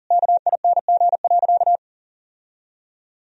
Morse code, heard as a keyed tone, about 40 wpm, 700 Hz. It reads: KING.
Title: Sentences from Top 1000 Words - Review Part 2 - 40wpm